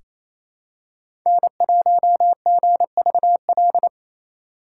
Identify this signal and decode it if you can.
Morse code, heard as a keyed tone, about 28 wpm, 700 Hz. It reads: N1GVL
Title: USA Call Signs - 5 Characters - 4x Character Spacing - 28wpm